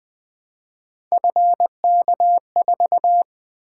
Morse code, heard as a keyed tone, about 20 wpm, 700 Hz. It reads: FK4
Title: Three Letter-Number Combinations - 20wpm